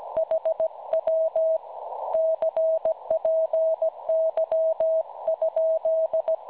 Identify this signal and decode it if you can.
Morse code, heard as a keyed tone, about 17 wpm, 640 Hz. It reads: HW CPY?
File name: cwronde / websdr recording .6kHz